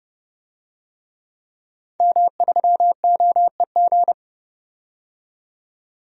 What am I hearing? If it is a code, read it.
Morse code, heard as a keyed tone, about 30 wpm, 700 Hz. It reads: M3OEG
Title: DXCC-100 Call Signs - 5 Characters - 30wpm